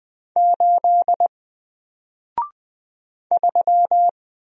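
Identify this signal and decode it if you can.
Morse code, heard as a keyed tone, about 20 wpm, 700 Hz. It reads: 8 3
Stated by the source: Single letter and number Morse drill at 20 wpm, from N4LCD.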